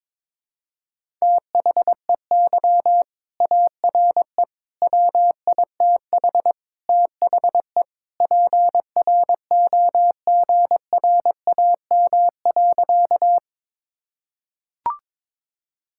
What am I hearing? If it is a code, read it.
Morse code, heard as a keyed tone, about 22 wpm, 700 Hz. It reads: THEY ARE WITH THE PROGRAM.
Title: Sentences from Top 1000 Words - Review Part 5 - 22wpm